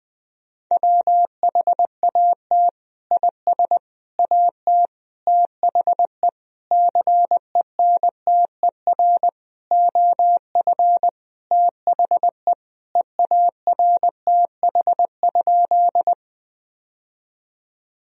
Morse code, heard as a keyed tone, about 20 wpm, 700 Hz. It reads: WHAT IS AT THE CENTER OF THE EARTH?
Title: Sentences from Top 1000 Words - Review Part 2 - 20wpm